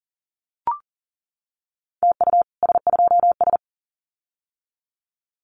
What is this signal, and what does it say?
Morse code, heard as a keyed tone, about 40 wpm, 700 Hz. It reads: TU S2S
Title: Common QSO Phrases Encore 40wpm